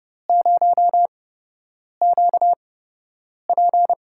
Morse code, heard as a keyed tone, about 30 wpm, spaced as for 8 wpm, 700 Hz. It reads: 0QP
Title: USA Call Signs - 8x Character Spacing - 30wpm